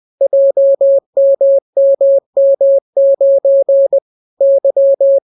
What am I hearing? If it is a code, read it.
Morse code, heard as a keyed tone, about 20 wpm, 550 Hz. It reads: JMMM9 Y